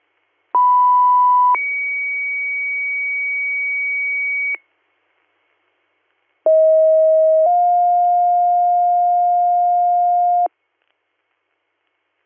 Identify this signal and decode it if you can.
Two-tone page, heard as a one-second tone, then a three-second tone, about 990 then 2250 Hz, about 640 then 710 Hz. It is heard from 0.5 to 10.5 s.